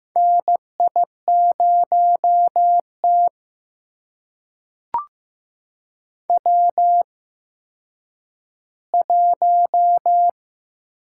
Morse code, heard as a keyed tone, about 15 wpm, spaced as for 9 wpm, 700 Hz. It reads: NI0T W1
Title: USA Call Signs - 4 Characters - 8x Character Spacing - 15wpm